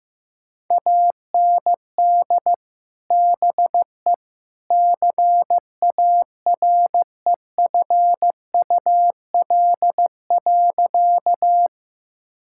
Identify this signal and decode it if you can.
Morse code, heard as a keyed tone, about 15 wpm, 700 Hz. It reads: AND BE CAREFUL.
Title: Sentences from Top 1600 Words 15wpm